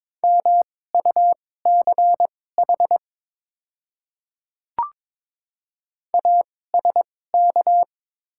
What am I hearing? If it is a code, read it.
Morse code, heard as a keyed tone, about 22 wpm, spaced as for 16 wpm, 700 Hz. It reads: MUCH ASK
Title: 2x Character Spacing - Top 200 Words - 22wpm